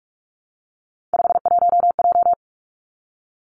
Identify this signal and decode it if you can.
Morse code, heard as a keyed tone, about 46 wpm, 700 Hz. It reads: 51J
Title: Three Challenging Letter-Number Combinations 45wpm